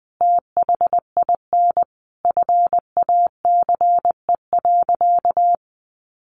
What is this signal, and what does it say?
Morse code, heard as a keyed tone, about 19 wpm, 700 Hz. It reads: THIN FACE.
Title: A Sound Of Thunder 20wpm